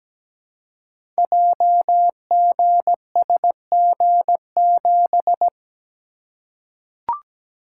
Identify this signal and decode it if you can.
Morse code, heard as a keyed tone, about 17 wpm, 700 Hz. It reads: JGSG7
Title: Code Groups - 5 Letter:Numbers - 17wpm